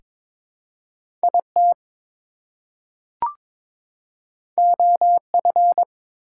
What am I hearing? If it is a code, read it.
Morse code, heard as a keyed tone, about 22 wpm, 700 Hz. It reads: IT OF
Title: Two Letter Words - 22wpm